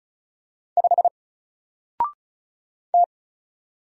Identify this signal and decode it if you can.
Morse code, heard as a keyed tone, about 36 wpm, 700 Hz. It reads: 5 T
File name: Single-Letter-Number-35wpm